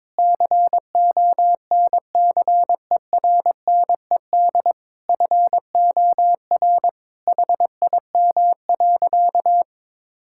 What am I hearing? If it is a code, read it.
Morse code, heard as a keyed tone, about 22 wpm, 700 Hz. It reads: CONCERNED FOR HIM.
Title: Sentences from Top 1000 Words - Review Part 6 - 22wpm